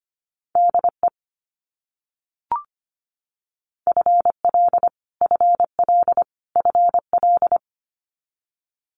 Morse code, heard as a keyed tone, about 25 wpm, 700 Hz. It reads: DE FL FL FL
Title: US States Repeated 3x 25wpm